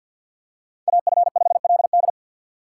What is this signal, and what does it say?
Morse code, heard as a keyed tone, about 51 wpm, 700 Hz. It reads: AF5LD
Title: USA Call Signs - 5 Characters - 3x Character Spacing - 50wpm